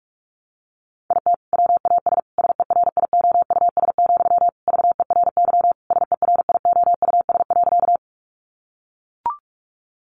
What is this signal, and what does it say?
Morse code, heard as a keyed tone, about 44 wpm, 700 Hz. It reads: IT WAS SERIOUS, VERY SERIOUS.
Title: Sentences from Top 1100 Words 45wpm